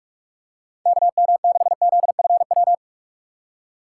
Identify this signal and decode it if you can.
Morse code, heard as a keyed tone, about 43 wpm, 700 Hz. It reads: KM6ZFW
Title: USA Call Signs - 6 Characters - 6x Character Spacing - 45wpm